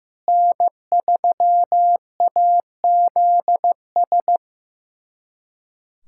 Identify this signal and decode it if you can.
Morse code, heard as a keyed tone, about 15 wpm, 700 Hz. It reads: N3AZS
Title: USA Callsign DE Callsign - 5 Characters - 15wpm